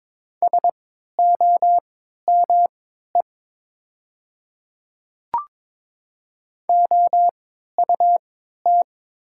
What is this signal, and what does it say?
Morse code, heard as a keyed tone, about 22 wpm, spaced as for 12 wpm, 700 Hz. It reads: SOME OUT
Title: 3x Character Spacing - Top 100 Words - 22wpm